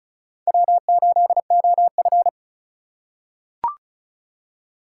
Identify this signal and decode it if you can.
Morse code, heard as a keyed tone, about 36 wpm, 700 Hz. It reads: W8OF
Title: USA Call Signs - 2x Character Spacing - 35wpm